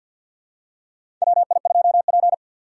Morse code, heard as a keyed tone, about 52 wpm, 700 Hz. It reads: WI2P
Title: USA Call Signs - 4 Characters - 6x Character Spacing - 50wpm